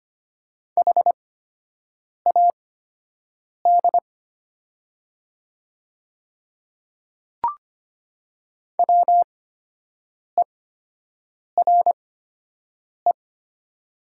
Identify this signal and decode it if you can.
Morse code, heard as a keyed tone, about 25 wpm, spaced as for 7 wpm, 700 Hz. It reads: HAD WERE
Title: 8x Character Spacing - Top 100 Words - 25wpm